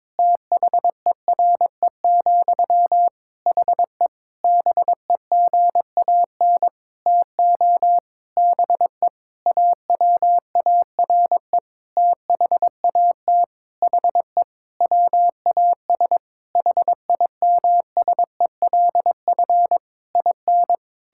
Morse code, heard as a keyed tone, about 22 wpm, 700 Hz. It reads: THERE, HE BEGAN TO BE AWARE THAT HE WAS HIMSELF IN
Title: Sentences from Top 1600 Words 22wpm